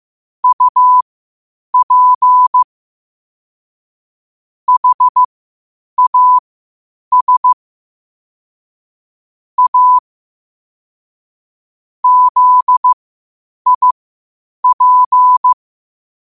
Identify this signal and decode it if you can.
Morse code, heard as a keyed tone, about 15 wpm, spaced as for 8 wpm, 1000 Hz. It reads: UP HAS A ZIP
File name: qotdfiles / 8wpm